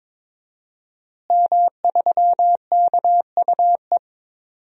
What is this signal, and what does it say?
Morse code, heard as a keyed tone, about 22 wpm, 700 Hz. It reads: M3KUE
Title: DXCC-100 Call Signs - 5 Characters - Repeated 2x - 22wpm